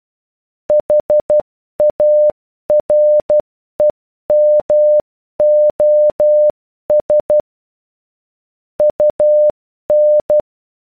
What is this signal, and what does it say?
Morse code, heard as a keyed tone, about 12 wpm, 600 Hz. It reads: HAREMOS UN